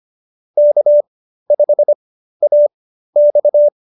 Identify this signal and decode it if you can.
Morse code, heard as a keyed tone, about 25 wpm, spaced as for 13 wpm, 590 Hz. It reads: K5AX